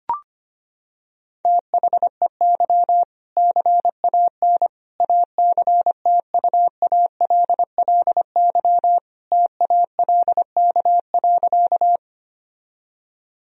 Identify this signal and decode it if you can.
Morse code, heard as a keyed tone, about 25 wpm, 700 Hz. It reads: THEY CAN ACTUALLY TALK.
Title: Sentences from Top 1000 Words - Review Part 4 - 25wpm